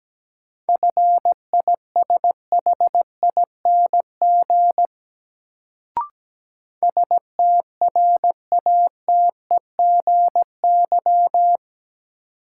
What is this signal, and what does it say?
Morse code, heard as a keyed tone, about 17 wpm, 700 Hz. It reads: FISHING STRATEGY
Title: Sentences from Top 1000 Words - Review Part 5 - New Word Drill - 17wpm